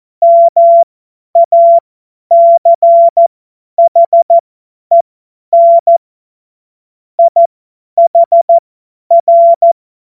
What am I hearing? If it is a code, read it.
Morse code, heard as a keyed tone, about 14 wpm, spaced as for 10 wpm, 680 Hz. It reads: MACHEN IHR